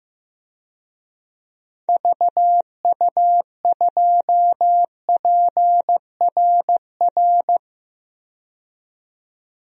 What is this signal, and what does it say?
Morse code, heard as a keyed tone, about 15 wpm, 700 Hz. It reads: VU2PRR
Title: DXCC-100 Call Signs - 6 Characters - 15wpm